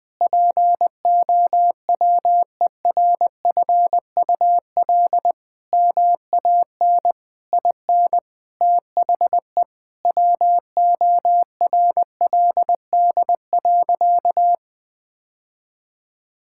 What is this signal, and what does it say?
Morse code, heard as a keyed tone, about 20 wpm, 700 Hz. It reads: POWERFUL MAN IN THE WORLD.
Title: Sentences with 7-Word Phrases - 20wpm